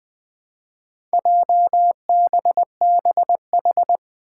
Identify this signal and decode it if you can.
Morse code, heard as a keyed tone, about 20 wpm, 700 Hz. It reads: JBBH
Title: Four Challenging Letter-Number Combinations 20wpm